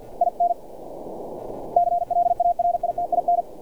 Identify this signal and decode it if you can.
Morse code, heard as a keyed tone, about 50 wpm, 700 Hz. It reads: IN CLARITIN
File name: archived / CRSnet